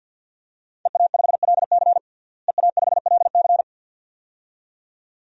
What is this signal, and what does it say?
Morse code, heard as a keyed tone, about 52 wpm, 700 Hz. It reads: EA5LC EA5LC
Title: DXCC-100 Call Signs - 5 Characters - Repeated 2x - 50wpm